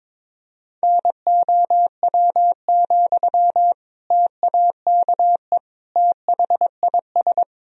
Morse code, heard as a keyed tone, about 22 wpm, 700 Hz. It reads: NOW, TAKE THIS
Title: Sentences from Top 400 Words and Abbreviations 22wpm